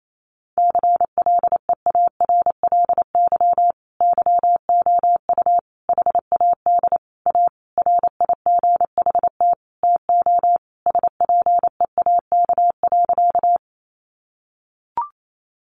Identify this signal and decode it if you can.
Morse code, heard as a keyed tone, about 28 wpm, 700 Hz. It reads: CLEARLY YOU HAD A RIGHT TO SPEAK.
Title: Sentences from Top 1300 Words 28wpm